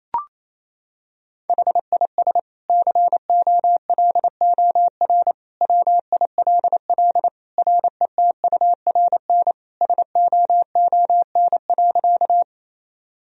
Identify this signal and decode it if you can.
Morse code, heard as a keyed tone, about 28 wpm, 700 Hz. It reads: HIS COLOR WILL RETURN SOON.
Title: Sentences from Top 1000 Words - Review Part 2 - 28wpm